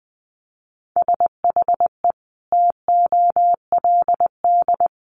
Morse code, heard as a keyed tone, about 20 wpm, 700 Hz. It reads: SHE TOLD
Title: Sentences from Top 1500 Words 20wpm